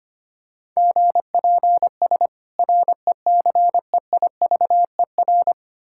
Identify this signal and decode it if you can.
Morse code, heard as a keyed tone, about 25 wpm, 700 Hz. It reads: GPS RECEIVER
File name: practice / 25wpm / Difficult Words - Speed-Racing with Hints - 25wpm